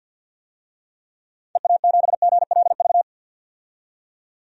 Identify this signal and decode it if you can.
Morse code, heard as a keyed tone, about 51 wpm, 700 Hz. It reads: EA7GLV
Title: DXCC-100 Call Signs - 6 Characters - 50wpm